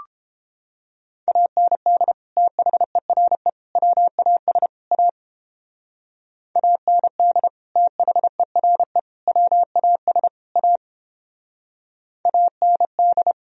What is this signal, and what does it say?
Morse code, heard as a keyed tone, about 31 wpm, 700 Hz. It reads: AND THERE WAS A AND THERE WAS A AND